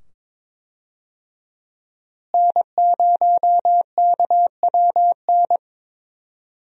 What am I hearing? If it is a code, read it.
Morse code, heard as a keyed tone, about 21 wpm, 700 Hz. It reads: N0KWN
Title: USA Call Signs - 5 Characters - Repeated 2x - 22wpm